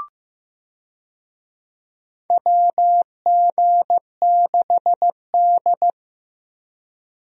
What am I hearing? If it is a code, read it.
Morse code, heard as a keyed tone, about 15 wpm, 700 Hz. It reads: WG6D
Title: USA Call Signs - 4 Characters - 15wpm